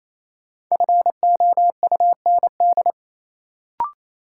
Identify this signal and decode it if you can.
Morse code, heard as a keyed tone, about 28 wpm, 700 Hz. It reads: FOUND
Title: Sentences from Top 300 Words - New Word Drill - 28wpm